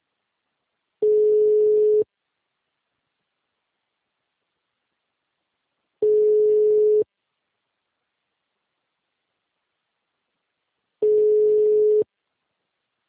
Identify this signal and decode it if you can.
Ringback tone (Europe, 425 Hz).